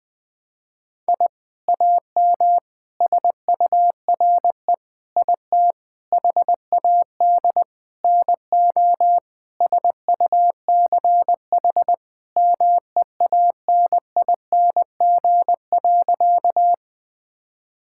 Morse code, heard as a keyed tone, about 20 wpm, 700 Hz. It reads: I AM SURE IT HAD NO SUCH MEANING.
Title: Sentences from Top 1000 Words - Review Part 6 - 20wpm